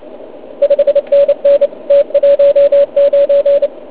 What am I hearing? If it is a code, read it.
Morse code, heard as a keyed tone, about 27 wpm, 570 Hz. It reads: 5NN T19
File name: Pile UP CUC204